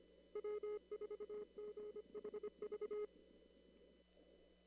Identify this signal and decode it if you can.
Morse code, heard as a keyed tone, about 25 wpm, 420 Hz. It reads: W4GHV
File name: CW / kt5j wpx cw-op-k5pi